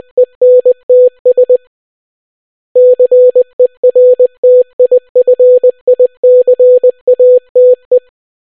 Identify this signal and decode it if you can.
Morse code, heard as a keyed tone, about 20 wpm, 500 Hz. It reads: ENTS CERTIFICATE